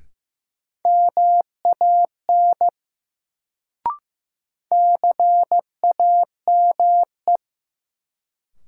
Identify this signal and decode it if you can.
Morse code, heard as a keyed tone, about 15 wpm, 700 Hz. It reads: MAN CAME
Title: Sentences from Top 200 Words - New Word Drill - 15wpm